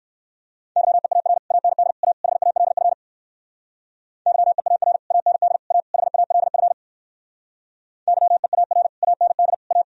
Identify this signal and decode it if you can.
Morse code, heard as a keyed tone, about 59 wpm, 700 Hz. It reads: YEAR AND A HALF YEAR AND A HALF YEAR AND A